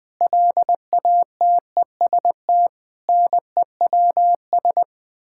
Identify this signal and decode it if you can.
Morse code, heard as a keyed tone, about 20 wpm, 700 Hz. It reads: LATEST NEWS
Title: Sets of 9 Words - 20wpm